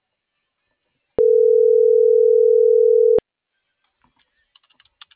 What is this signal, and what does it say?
Ringback tone (North America), 440 + 480 Hz.